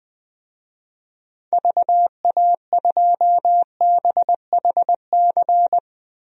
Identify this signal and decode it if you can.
Morse code, heard as a keyed tone, about 20 wpm, 700 Hz. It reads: VA2BHC
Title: DXCC-100 Call Signs - 6 Characters - Repeated 2x - 20wpm